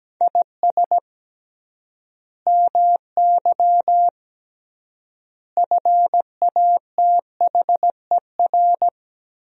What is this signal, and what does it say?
Morse code, heard as a keyed tone, about 17 wpm, 700 Hz. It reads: IS MY FATHER
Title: 3x Word-Spacing - Sentences from Top 500 Words - 17wpm